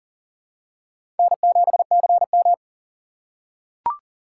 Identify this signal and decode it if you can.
Morse code, heard as a keyed tone, about 40 wpm, 700 Hz. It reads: N7CM
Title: USA Call Signs - 4 Characters - 2x Character Spacing - 40wpm